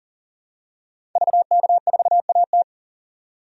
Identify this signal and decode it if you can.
Morse code, heard as a keyed tone, about 40 wpm, 700 Hz. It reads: VK4AT
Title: DXCC-100 Call Signs - 5 Characters - 40wpm